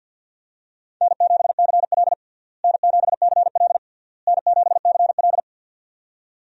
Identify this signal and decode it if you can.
Morse code, heard as a keyed tone, about 51 wpm, 700 Hz. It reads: N7CL N7CL N7CL
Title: USA Call Signs - 4 Characters - Repeated 3x - 50wpm